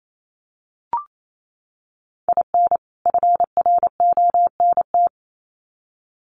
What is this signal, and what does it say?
Morse code, heard as a keyed tone, about 28 wpm, 700 Hz. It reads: IN FRONT
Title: Sets of 2 Words 28wpm